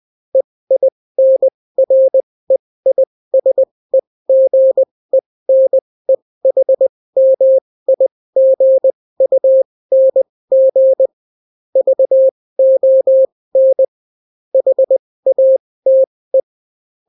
Morse code, heard as a keyed tone, about 20 wpm, spaced as for 16 wpm, 530 Hz. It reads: EINREISEGENEHMIGUNG VON HATE